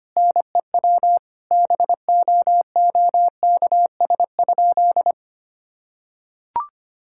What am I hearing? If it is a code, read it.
Morse code, heard as a keyed tone, about 25 wpm, 700 Hz. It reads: NEW BOOKS?
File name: practice / 25wpm / Sentences with 9-Word Phrases - 25wpm